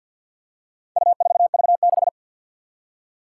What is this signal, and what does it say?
Morse code, heard as a keyed tone, about 48 wpm, 700 Hz. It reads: U4V6
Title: Four Challenging Letter-Number Combinations 50wpm